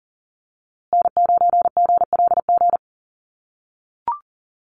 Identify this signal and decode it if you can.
Morse code, heard as a keyed tone, about 40 wpm, 700 Hz. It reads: N9GLG
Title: USA Call Signs Repeated 2x 40wpm